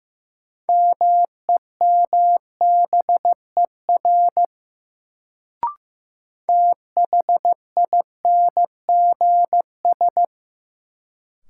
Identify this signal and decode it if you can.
Morse code, heard as a keyed tone, about 15 wpm, 700 Hz. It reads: MEMBER THINGS
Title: Sets of 3 Words - New Word Drill - 15wpm